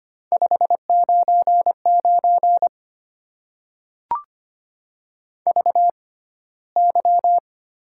Morse code, heard as a keyed tone, about 25 wpm, spaced as for 15 wpm, 700 Hz. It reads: 599 VY
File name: Top 90 QSO Elements - 6x Character Spacing - 25wpm